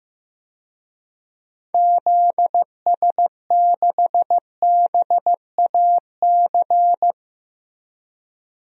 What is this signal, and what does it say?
Morse code, heard as a keyed tone, about 15 wpm, 700 Hz. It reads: ZS6BAC
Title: DXCC-100 Call Signs - 6 Characters - Repeated 3x - 15wpm